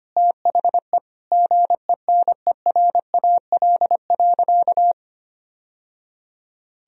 Morse code, heard as a keyed tone, about 25 wpm, 700 Hz. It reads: THE GENERAL.
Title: Sentences from Top 1000 Words - Review Part 3 - 25wpm